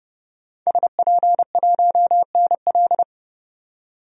Morse code, heard as a keyed tone, about 31 wpm, 700 Hz. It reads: SP1NL
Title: DXCC-100 Call Signs - 5 Characters - 30wpm